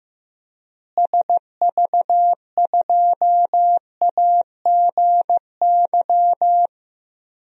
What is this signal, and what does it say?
Morse code, heard as a keyed tone, about 15 wpm, 700 Hz. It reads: SV2AGY
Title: DXCC-100 Call Signs - 6 Characters - Repeated 2x - 15wpm